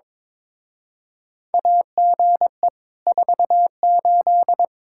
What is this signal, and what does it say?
Morse code, heard as a keyed tone, about 22 wpm, 700 Hz. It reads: AGE 48
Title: Speed-Racing Common QSO Phrases - 22wpm